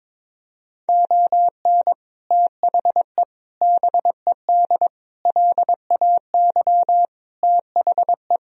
Morse code, heard as a keyed tone, about 22 wpm, 700 Hz. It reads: ON THE BED LAY THE
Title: Sentences from Top 1000 Words - Review Part 2 - 22wpm